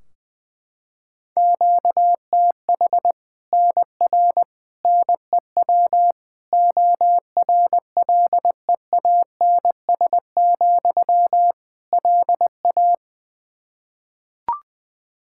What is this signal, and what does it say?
Morse code, heard as a keyed tone, about 21 wpm, 700 Hz. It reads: QTH NR NEW ORLEANS, LA
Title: Practice QSO - Exchange 1 - Beginning - 20wpm